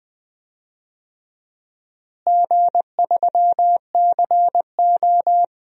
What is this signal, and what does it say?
Morse code, heard as a keyed tone, about 20 wpm, 700 Hz. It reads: G3CO
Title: DXCC-100 Call Signs - 4 Characters - Repeated 2x - 20wpm